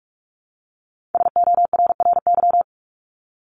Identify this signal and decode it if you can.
Morse code, heard as a keyed tone, about 44 wpm, 700 Hz. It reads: SORRY
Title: Top 1751-2000 Words 45wpm